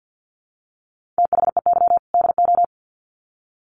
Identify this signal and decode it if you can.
Morse code, heard as a keyed tone, about 48 wpm, 700 Hz. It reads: THEY DO
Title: Sets of 2 Words 50wpm